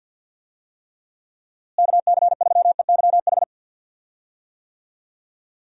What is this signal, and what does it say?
Morse code, heard as a keyed tone, about 51 wpm, 700 Hz. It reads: KC3EYH
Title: USA Call Signs - 6 Characters - Repeated 2x - 50wpm